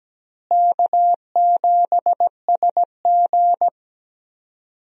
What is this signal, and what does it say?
Morse code, heard as a keyed tone, about 17 wpm, 700 Hz. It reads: K7SG
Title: USA Call Signs - 4 Characters - Repeated 3x - 17wpm